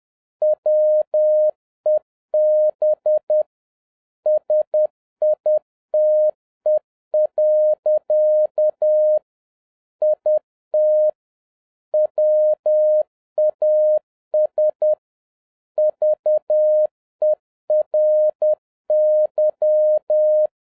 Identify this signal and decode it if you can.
Morse code, heard as a keyed tone, about 10 wpm, 610 Hz. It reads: WEB SITE. IT WAS VERY